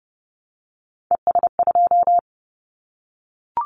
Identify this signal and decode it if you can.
Morse code, heard as a keyed tone, about 30 wpm, 700 Hz. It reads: ES2
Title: Callsign Prefixes for Top 50 DXCC - 30wpm